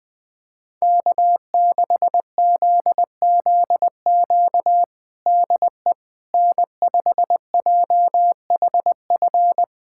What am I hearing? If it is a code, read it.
Morse code, heard as a keyed tone, about 20 wpm, 700 Hz. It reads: K6ZZQ DE N5JHF
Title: USA Callsign DE Callsign - 5 Characters - 20wpm